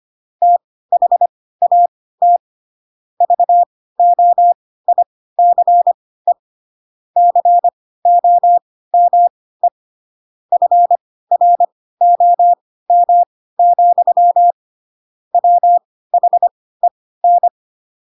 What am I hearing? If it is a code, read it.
Morse code, heard as a keyed tone, about 25 wpm, spaced as for 16 wpm, 700 Hz. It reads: THAT VOICE COME FROM, WHEN
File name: Pinocchio / Pinocchio ch1 10 25 ik1hge